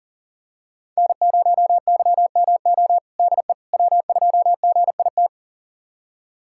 Morse code, heard as a keyed tone, about 40 wpm, 700 Hz. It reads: N0YMO DE W2GIT
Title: USA Callsign DE Callsign - 5 Characters - 40wpm